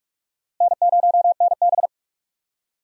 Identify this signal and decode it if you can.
Morse code, heard as a keyed tone, about 42 wpm, 700 Hz. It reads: N0NB
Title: USA Call Signs - 4 Characters - Repeated 3x - 45wpm